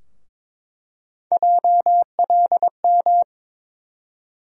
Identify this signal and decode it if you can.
Morse code, heard as a keyed tone, about 21 wpm, 700 Hz. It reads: JLM
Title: USA Call Sign Postfixes - 3 Letters 22wpm